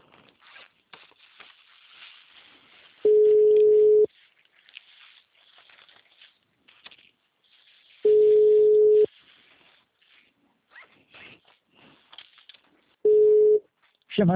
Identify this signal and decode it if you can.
Ringback tone (Europe, 425 Hz), with speech.